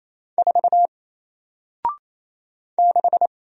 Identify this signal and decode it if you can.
Morse code, heard as a keyed tone, about 28 wpm, 700 Hz. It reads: 4 6
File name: Single-Letter-Number-28wpm